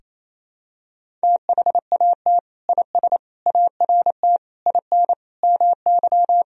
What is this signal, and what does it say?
Morse code, heard as a keyed tone, about 28 wpm, 700 Hz. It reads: THAT IS ART IN MY
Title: Sentences from Top 1000 Words - Review Part 3 - 28wpm